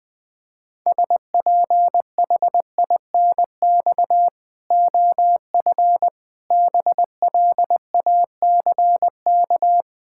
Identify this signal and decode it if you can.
Morse code, heard as a keyed tone, about 20 wpm, 700 Hz. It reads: SPHINX OF BLACK
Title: Pangram Challenge v2 - 20wpm